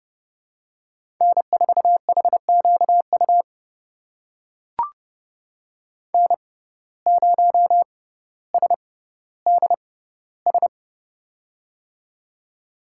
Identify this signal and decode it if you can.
Morse code, heard as a keyed tone, about 30 wpm, spaced as for 15 wpm, 700 Hz. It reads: N4HQU N0SDS
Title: USA Call Signs - 5 Characters - 6x Character Spacing - 30wpm